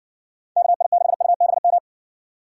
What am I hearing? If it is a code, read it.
Morse code, heard as a keyed tone, about 61 wpm, 700 Hz. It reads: KI6UBM